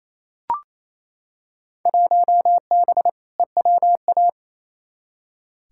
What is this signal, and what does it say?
Morse code, heard as a keyed tone, about 28 wpm, 700 Hz. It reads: 1B EWA
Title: Field Day Exchanges Encore - 28wpm